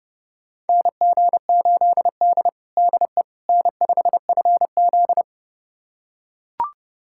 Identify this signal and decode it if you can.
Morse code, heard as a keyed tone, about 32 wpm, 700 Hz. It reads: NG8D DE N5FZ
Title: USA Callsign DE Callsign - 4 Characters - 30wpm